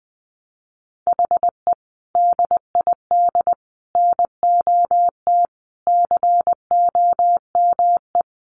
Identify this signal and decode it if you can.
Morse code, heard as a keyed tone, about 20 wpm, 700 Hz. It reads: HE DID NOT COME
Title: Sentences from Top 1200 Words 20wpm